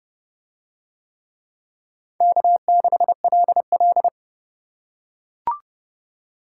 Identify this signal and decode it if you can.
Morse code, heard as a keyed tone, about 26 wpm, 700 Hz. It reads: K6LL